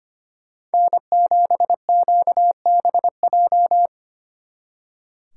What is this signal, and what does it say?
Morse code, heard as a keyed tone, about 29 wpm, 700 Hz. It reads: N7QBJ
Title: Speed Racing - USA Call Signs 25wpm